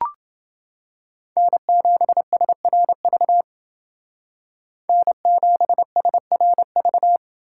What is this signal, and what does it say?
Morse code, heard as a keyed tone, about 28 wpm, 700 Hz. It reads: N7SRV N7SRV